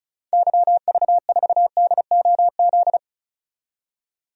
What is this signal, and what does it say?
Morse code, heard as a keyed tone, about 34 wpm, 700 Hz. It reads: YV4DOZ
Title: DXCC-100 Call Signs - 6 Characters - 35wpm